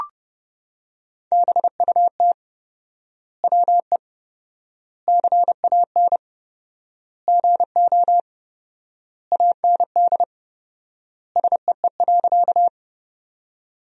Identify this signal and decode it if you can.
Morse code, heard as a keyed tone, about 30 wpm, 700 Hz. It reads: BUT WE CAN GO AND SEE.
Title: 4x Word-Spacing - Sentences from Top 100 Words - 30wpm